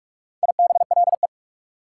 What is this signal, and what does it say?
Morse code, heard as a keyed tone, about 48 wpm, 690 Hz. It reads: IBLE
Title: Prefixes and Suffixes 45wpm